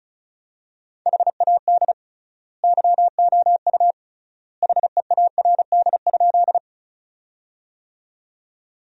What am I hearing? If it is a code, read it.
Morse code, heard as a keyed tone, about 34 wpm, 700 Hz. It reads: HAD YOU HEARD?
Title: 3x Word-Spacing - Sentences from Top 400 Words - 35wpm